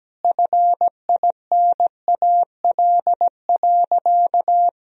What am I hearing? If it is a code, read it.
Morse code, heard as a keyed tone, about 17 wpm, 700 Hz. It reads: FINAL.
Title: Sentences from Top 1000 Words - Review Part 2 - 17wpm